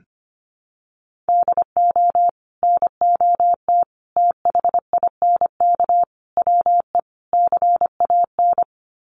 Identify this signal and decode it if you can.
Morse code, heard as a keyed tone, about 25 wpm, 700 Hz. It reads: DO NOT THINK WE CAN
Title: Sets of 5 Words Encore2 25wpm